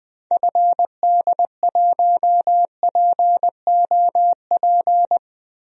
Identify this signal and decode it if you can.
Morse code, heard as a keyed tone, about 20 wpm, 700 Hz. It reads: FD1POP
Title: DXCC-100 Call Signs - 6 Characters - Repeated 2x - 20wpm